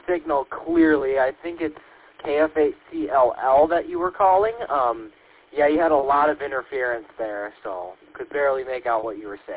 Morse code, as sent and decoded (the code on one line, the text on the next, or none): none